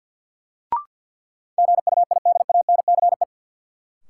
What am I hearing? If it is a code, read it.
Morse code, heard as a keyed tone, about 51 wpm, 700 Hz. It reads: GUIDANCE
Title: Sentences from Top 1600 Words - New Word Drill - 50wpm